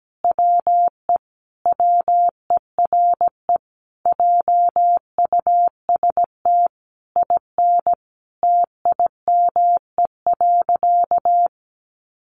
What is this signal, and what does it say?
Morse code, heard as a keyed tone, about 17 wpm, 700 Hz. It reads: WE WERE JUST IN TIME.
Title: Sentences from Top 300 Words 17wpm